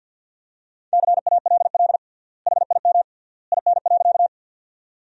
Morse code, heard as a keyed tone, about 50 wpm, 700 Hz. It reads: CALL HIM IN.
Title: 3x Word-Spacing - Sentences from Top 100 Words - 50wpm